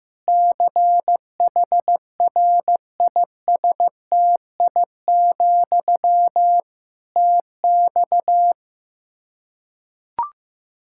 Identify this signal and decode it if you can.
Morse code, heard as a keyed tone, about 15 wpm, 700 Hz. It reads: CHRISTI, TX
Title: Practice QSO - Exchange 1 - Beginning - 15wpm